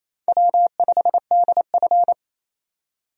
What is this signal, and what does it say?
Morse code, heard as a keyed tone, about 28 wpm, 700 Hz. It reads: W5DF
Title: USA Call Signs - 4 Characters - 28wpm